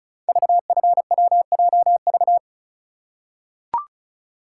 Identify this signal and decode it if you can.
Morse code, heard as a keyed tone, about 36 wpm, 700 Hz. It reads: VFWJV
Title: Code Groups - 5 Letters - 35wpm